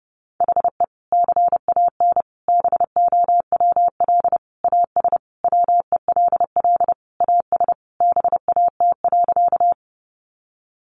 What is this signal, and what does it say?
Morse code, heard as a keyed tone, about 30 wpm, 700 Hz. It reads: HE CAN BOWL AS WELL AS BAT.
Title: Sentences from Top 1500 Words 30wpm